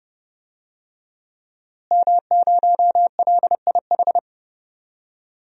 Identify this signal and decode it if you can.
Morse code, heard as a keyed tone, about 30 wpm, 700 Hz. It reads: M0LIH
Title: DXCC-100 Call Signs - 5 Characters - Repeated 2x - 30wpm